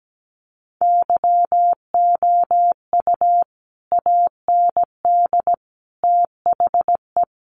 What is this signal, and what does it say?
Morse code, heard as a keyed tone, about 17 wpm, 700 Hz. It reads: YOU AND THE
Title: Sentences from Top 1300 Words 17wpm